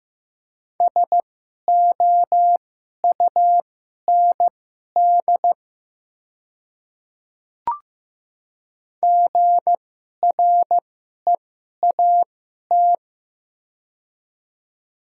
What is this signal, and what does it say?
Morse code, heard as a keyed tone, about 15 wpm, spaced as for 11 wpm, 700 Hz. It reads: SOUND GREAT
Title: 2x Character Spacing - Top 200 Words - 15wpm